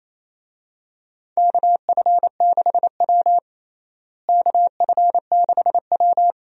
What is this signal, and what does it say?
Morse code, heard as a keyed tone, about 28 wpm, 700 Hz. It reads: KF6W KF6W
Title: USA Call Signs - 4 Characters - Repeated 2x - 28wpm